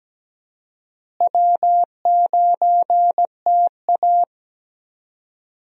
Morse code, heard as a keyed tone, about 17 wpm, 700 Hz. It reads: W9TA
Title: USA Call Signs - 4 Characters - Repeated 3x - 17wpm